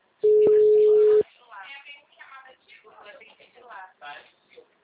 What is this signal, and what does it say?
Ringback tone (Europe, 425 Hz), with speech.